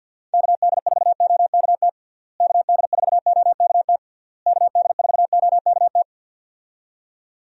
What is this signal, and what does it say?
Morse code, heard as a keyed tone, about 51 wpm, 700 Hz. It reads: KD4OKT KD4OKT KD4OKT
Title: USA Call Signs - 6 Characters - Repeated 3x - 50wpm